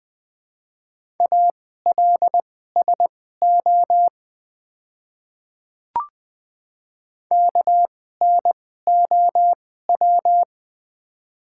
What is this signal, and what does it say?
Morse code, heard as a keyed tone, about 20 wpm, spaced as for 15 wpm, 690 Hz. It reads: ALSO KNOW
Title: 2x Character Spacing - Top 200 Words - 20wpm